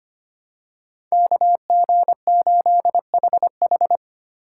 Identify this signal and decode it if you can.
Morse code, heard as a keyed tone, about 25 wpm, 700 Hz. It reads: KG8HH
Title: USA Call Signs - 5 Characters - 4x Character Spacing - 25wpm